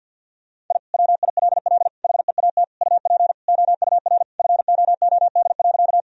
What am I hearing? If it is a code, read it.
Morse code, heard as a keyed tone, about 48 wpm, 700 Hz. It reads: I WILL HEAT UP OUR FOOD.
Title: Sentences from Top 1000 Words - Review Part 3 - 50wpm